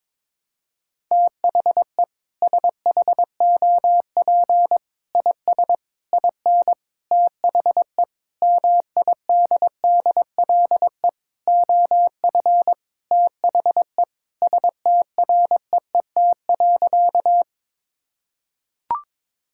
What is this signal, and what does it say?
Morse code, heard as a keyed tone, about 21 wpm, 700 Hz. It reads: THE SHOP IS IN THE MIDDLE OF THE STREET.
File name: practice / 22wpm / Sentences with 4-Word Phrases - 22wpm